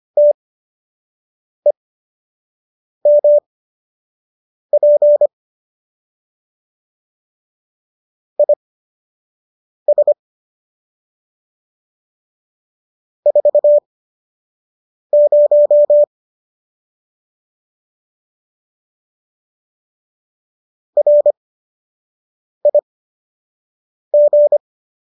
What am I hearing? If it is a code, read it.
Morse code, heard as a keyed tone, about 25 wpm, spaced as for 6 wpm, 590 Hz. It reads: TEMP IS 40 RIG